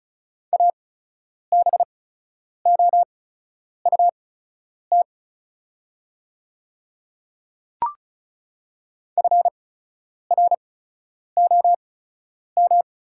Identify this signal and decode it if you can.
Morse code, heard as a keyed tone, about 36 wpm, spaced as for 10 wpm, 700 Hz. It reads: ABOUT FROM
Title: 8x Character Spacing - Top 300 Words - 35wpm